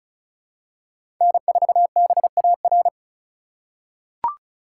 Morse code, heard as a keyed tone, about 36 wpm, 700 Hz. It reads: N4BAR
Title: USA Call Signs - 5 Characters - 3x Character Spacing - 35wpm